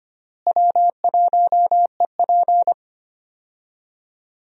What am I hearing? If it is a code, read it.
Morse code, heard as a keyed tone, about 25 wpm, 700 Hz. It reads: W1EP
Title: USA Call Signs - 4 Characters - Repeated 3x - 25wpm